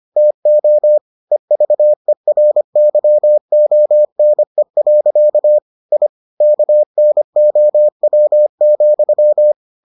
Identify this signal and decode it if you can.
Morse code, heard as a keyed tone, about 25 wpm, 600 Hz. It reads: TO EVERYONE. I KNOW,